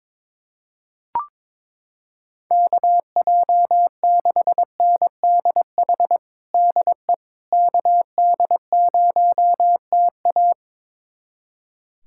Morse code, heard as a keyed tone, about 22 wpm, 700 Hz. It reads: KJ6NDH DE KD0TA
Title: USA Callsign DE Callsign 22wpm